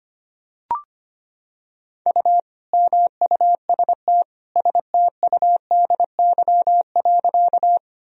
Morse code, heard as a keyed tone, about 25 wpm, 700 Hz. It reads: U MUST STUDY.
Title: Sentences from Top 300 Words and Abbreviations 25wpm